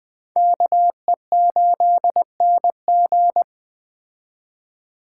Morse code, heard as a keyed tone, about 20 wpm, 700 Hz. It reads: KE8NG
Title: Speed Racing - USA 2x2 Call Signs 17wpm